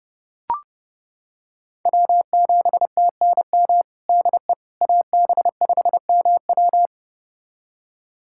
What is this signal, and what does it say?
Morse code, heard as a keyed tone, about 30 wpm, 700 Hz. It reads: W7TNM DE AB5MW
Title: USA Callsign DE Callsign - 5 Characters - 30wpm